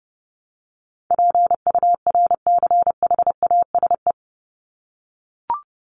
Morse code, heard as a keyed tone, about 30 wpm, 700 Hz. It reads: PURCHASE
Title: Top 501-750 Words 30wpm